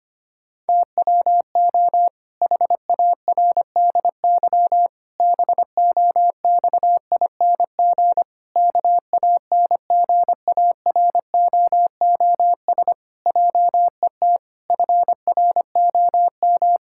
Morse code, heard as a keyed tone, about 25 wpm, 700 Hz. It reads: TWO HARDY BOXING KANGAROOS JET FROM
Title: Pangram Challenge v2 - 25wpm